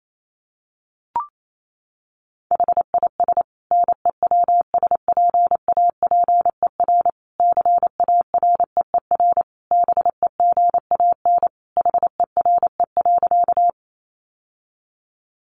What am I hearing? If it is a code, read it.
Morse code, heard as a keyed tone, about 28 wpm, 700 Hz. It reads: HIS NEWSPAPER CAREER BEGAN HERE.
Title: Sentences from Top 1100 Words 28wpm